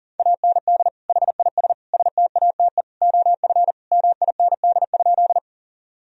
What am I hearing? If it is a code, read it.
Morse code, heard as a keyed tone, about 37 wpm, 700 Hz. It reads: AND HIS STATE OF MIND?
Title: Sentences from Top 1000 Words - Review Part 2 - 40wpm